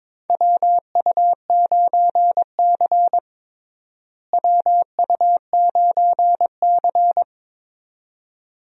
Morse code, heard as a keyed tone, about 22 wpm, 700 Hz. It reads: WU9C WU9C
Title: USA Call Signs - 4 Characters - Repeated 2x - 22wpm